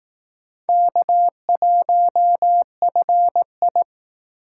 Morse code, heard as a keyed tone, about 19 wpm, 700 Hz. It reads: K1FI